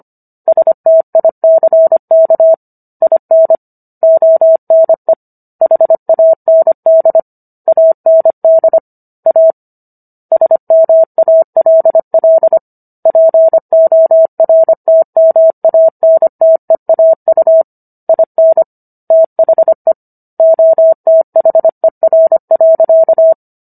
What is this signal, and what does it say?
Morse code, heard as a keyed tone, about 25 wpm, 650 Hz. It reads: STICK IN ONE HAND AND A SMALL PORTMANTEAU IN THE OTHER.